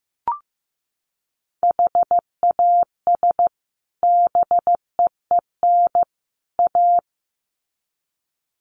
Morse code, heard as a keyed tone, about 15 wpm, 700 Hz. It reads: HAS BEEN A
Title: Sets of 3 Words 15wpm